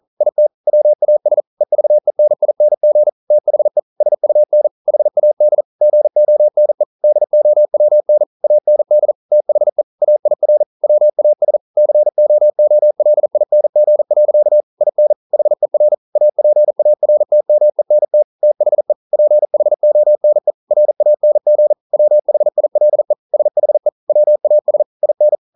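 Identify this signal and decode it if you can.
Morse code, heard as a keyed tone, about 42 wpm, 600 Hz. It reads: IT WAS EVENING THE SUN HAD GONE DOWN AND THE AIR WAS COOLING. IN HER APARTMENT THE PHONE RANG WHILE SHE WAS IN
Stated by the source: Morse code practice at 41 wpm, from CW med Gnister.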